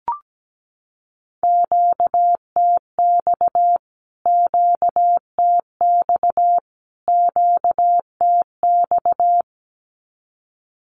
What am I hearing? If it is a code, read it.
Morse code, heard as a keyed tone, about 17 wpm, 700 Hz. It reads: QTX QTX QTX
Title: Q-Codes Repeated 3x 17wpm